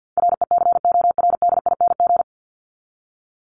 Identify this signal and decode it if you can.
Morse code, heard as a keyed tone, about 50 wpm, 700 Hz. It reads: RECORDING
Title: Top 1501-1750 Words 50wpm